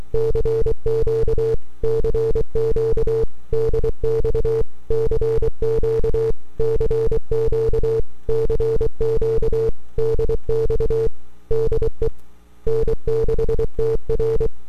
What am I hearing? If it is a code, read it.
Morse code, heard as a keyed tone, about 24 wpm, 450 Hz. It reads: CQ CQ DX CQ CQ CQ DX DE N6TR
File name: n6tr / 12Oct2009-160